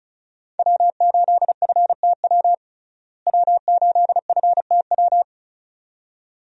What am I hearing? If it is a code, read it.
Morse code, heard as a keyed tone, about 34 wpm, 700 Hz. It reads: W8FTW W8FTW
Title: USA Call Signs - 5 Characters - Repeated 2x - 35wpm